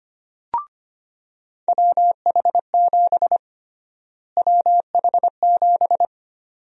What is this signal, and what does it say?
Morse code, heard as a keyed tone, about 25 wpm, 700 Hz. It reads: WH7 WH7
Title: USA Call Sign Prefixes - Repeated 2x - 25wpm